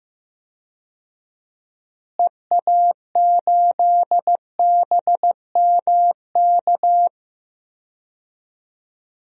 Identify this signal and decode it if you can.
Morse code, heard as a keyed tone, about 15 wpm, 700 Hz. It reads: EA8BMK
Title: DXCC-100 Call Signs - 6 Characters - Repeated 2x - 15wpm